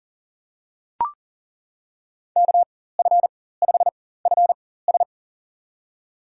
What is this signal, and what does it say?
Morse code, heard as a keyed tone, about 40 wpm, spaced as for 19 wpm, 700 Hz. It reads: KF5FS
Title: USA Call Signs - 5 Characters - 4x Character Spacing - 40wpm